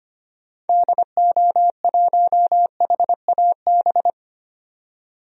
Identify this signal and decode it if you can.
Morse code, heard as a keyed tone, about 25 wpm, 700 Hz. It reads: DO1HAB
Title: DXCC-100 Call Signs - 6 Characters - 25wpm